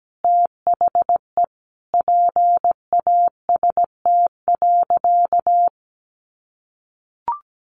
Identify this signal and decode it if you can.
Morse code, heard as a keyed tone, about 17 wpm, 700 Hz. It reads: THE PAST.
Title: A Sound Of Thunder 17wpm